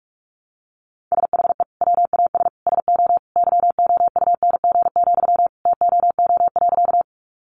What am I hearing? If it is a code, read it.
Morse code, heard as a keyed tone, about 44 wpm, 700 Hz. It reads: SHE WAS SO YOUNG, TOO.
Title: Sentences from Top 400 Words 45wpm